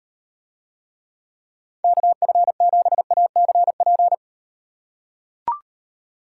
Morse code, heard as a keyed tone, about 38 wpm, 700 Hz. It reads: KF7ACP